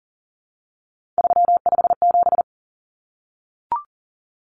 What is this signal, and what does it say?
Morse code, heard as a keyed tone, about 40 wpm, 700 Hz. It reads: 357
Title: Common Cut Numbers - 3-Digit 40wpm